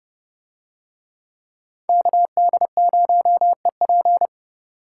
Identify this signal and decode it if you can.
Morse code, heard as a keyed tone, about 27 wpm, 700 Hz. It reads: KD0EP